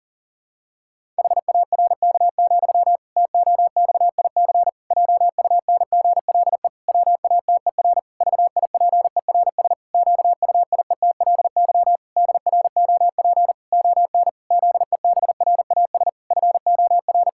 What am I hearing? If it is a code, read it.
Morse code, heard as a keyed tone, about 40 wpm, 700 Hz. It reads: HARK, TOXIC JUNGLE WATER VIPERS QUIETLY DROP ON ZEBRAS FOR
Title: Pangram Challenge v2 - 40wpm